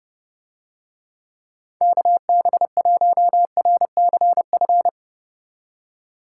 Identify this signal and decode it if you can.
Morse code, heard as a keyed tone, about 28 wpm, 700 Hz. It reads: KB1RCF